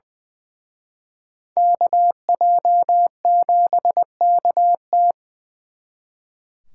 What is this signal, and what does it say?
Morse code, heard as a keyed tone, about 24 wpm, 700 Hz. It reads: KJ7KT
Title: Speed Racing - USA 2x2 Call Signs 20wpm